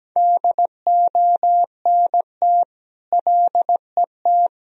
Morse code, heard as a keyed tone, about 17 wpm, 700 Hz. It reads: DONT LET
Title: Sentences from Top 1000 Words - Review Part 1 - 17wpm